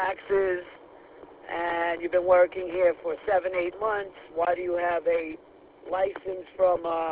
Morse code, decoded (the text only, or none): none